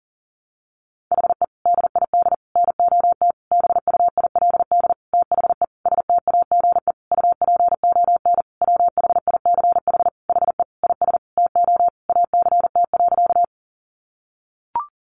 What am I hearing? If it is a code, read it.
Morse code, heard as a keyed tone, about 40 wpm, 700 Hz. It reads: HE DID NOT BUILD THE STAGE UPON WHICH HE IS TO ACT.
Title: Sentences from Top 1200 Words 40wpm